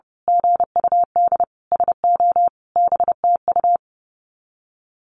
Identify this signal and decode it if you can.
Morse code, heard as a keyed tone, about 30 wpm, 700 Hz. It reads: GUD SO BTU
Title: Common QSO Phrases 30wpm